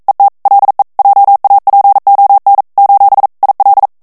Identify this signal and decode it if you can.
Morse code, heard as a keyed tone, about 42 wpm, 800 Hz. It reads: ET LE JAPON : IL
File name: FRANCAIS TEXTE CLAIR 1 VITESSE 40 MOTS MINUTES